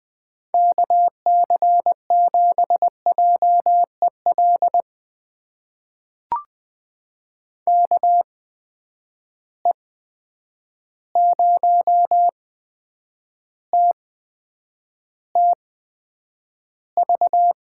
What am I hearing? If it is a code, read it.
Morse code, heard as a keyed tone, about 20 wpm, spaced as for 9 wpm, 700 Hz. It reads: KC7JEL KE0TTV
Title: USA Call Signs - 6 Characters - 8x Character Spacing - 20wpm